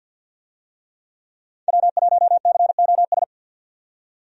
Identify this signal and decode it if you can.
Morse code, heard as a keyed tone, about 50 wpm, 700 Hz. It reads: W1COS
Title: USA Call Signs - 5 Characters - 50wpm